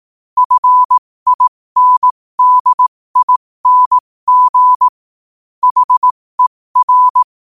Morse code, heard as a keyed tone, about 18 wpm, spaced as for 15 wpm, 1000 Hz. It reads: FINDING HER
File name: qotdfiles / 15wpm